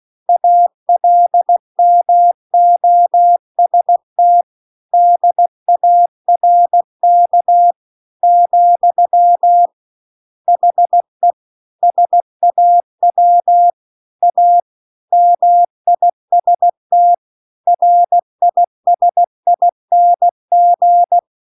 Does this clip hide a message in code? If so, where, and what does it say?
Yes: Morse code from 0.3 to 21.2 s, heard as a keyed tone, about 16 wpm, 700 Hz. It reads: ALMOST DARK , HE SAW A MIST RISING